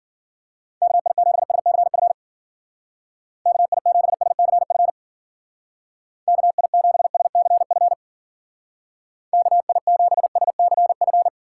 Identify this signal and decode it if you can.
Morse code, heard as a keyed tone, about 52 wpm, 700 Hz. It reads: KI7SCF KI7SCF KI7SCF KI7SCF